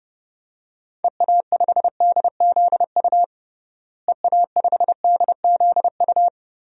Morse code, heard as a keyed tone, about 30 wpm, 700 Hz. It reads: EA5DZU EA5DZU
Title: DXCC-100 Call Signs - 6 Characters - Repeated 3x - 30wpm